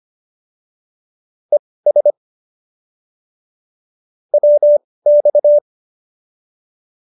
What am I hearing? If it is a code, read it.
Morse code, heard as a keyed tone, about 25 wpm, spaced as for 18 wpm, 590 Hz. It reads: ES WX